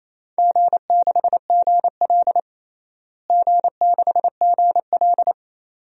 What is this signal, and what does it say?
Morse code, heard as a keyed tone, about 28 wpm, 700 Hz. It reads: G6GL G6GL
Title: DXCC-100 Call Signs - 4 Characters - Repeated 2x - 28wpm